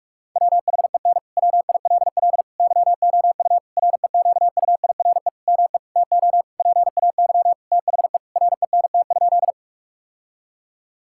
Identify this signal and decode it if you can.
Morse code, heard as a keyed tone, about 46 wpm, 700 Hz. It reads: WHEN WILL YOU REQUIRE ME TO PAY THE RENT?
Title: Sentences from Top 1000 Words - Review Part 6 - 45wpm